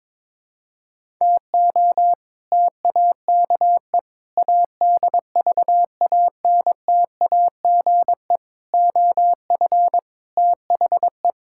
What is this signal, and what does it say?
Morse code, heard as a keyed tone, about 22 wpm, 700 Hz. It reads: TO TAKE ADVANTAGE OF THE
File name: practice / 22wpm / Sentences with 4-Word Phrases - 22wpm